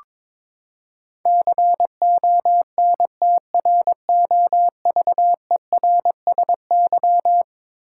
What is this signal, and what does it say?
Morse code, heard as a keyed tone, about 22 wpm, 700 Hz. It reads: CONTROVERSY
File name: practice / 22wpm / Eleven Letter Words - 22wpm